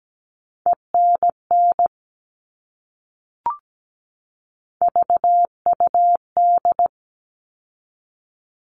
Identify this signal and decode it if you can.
Morse code, heard as a keyed tone, about 17 wpm, 700 Hz. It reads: ENN VUD
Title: All Cut Numbers - 3-Digit 17wpm